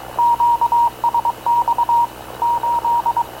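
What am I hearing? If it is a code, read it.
Morse code, heard as a keyed tone, about 22 wpm, 940 Hz. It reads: QSX 8
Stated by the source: Havana Radio / CLA sending Spanish press (PX) in Morse, recorded by Radio Officer Paul Dumesnil onboard CNAV QUEST/CZDO.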